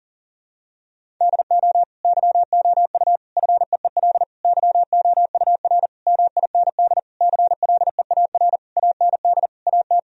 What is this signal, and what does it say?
Morse code, heard as a keyed tone, about 37 wpm, 700 Hz. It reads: DO YOU FEEL YOUR MIND CLEAR AND AT
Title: Sentences from Top 1000 Words - Review Part 1 - 40wpm